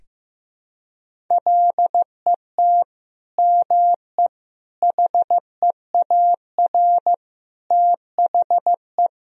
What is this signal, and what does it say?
Morse code, heard as a keyed tone, about 15 wpm, 700 Hz. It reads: LET ME HEAR THE
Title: Sentences from Top 1000 Words - Review Part 1 - 15wpm